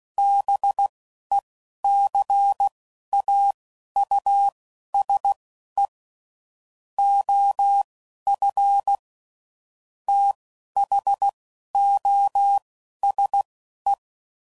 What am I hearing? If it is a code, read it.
Morse code, heard as a keyed tone, about 16 wpm, spaced as for 11 wpm, 800 Hz. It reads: BECAUSE OF THOSE
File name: CW Code Groups @10WPM/Technical Subjects